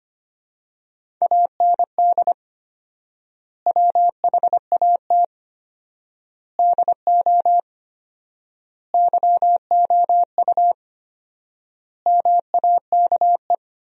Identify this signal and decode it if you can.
Morse code, heard as a keyed tone, about 25 wpm, 700 Hz. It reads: AND WHAT DO YOU MAKE
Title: 4x Word-Spacing - Sentences from Top 100 Words - 25wpm